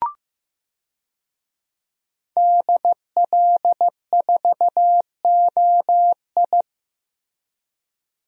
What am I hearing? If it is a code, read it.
Morse code, heard as a keyed tone, about 16 wpm, 700 Hz. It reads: DL4OI
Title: USA Call Signs - 5 Characters - Repeated 2x - 15wpm